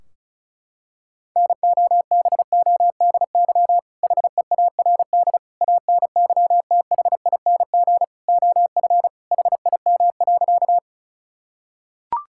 Morse code, heard as a keyed tone, about 36 wpm, 700 Hz. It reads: NOBODY HEARD ANYTHING OF HIM.
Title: Sentences from Top 1600 Words 35wpm